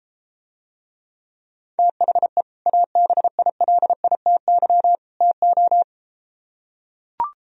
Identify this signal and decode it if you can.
Morse code, heard as a keyed tone, about 33 wpm, 700 Hz. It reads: THE ABILITY TO